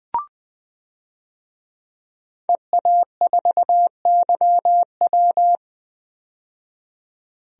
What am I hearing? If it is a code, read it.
Morse code, heard as a keyed tone, about 20 wpm, 700 Hz. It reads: EA4YW
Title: DXCC-100 Call Signs - 5 Characters - 20wpm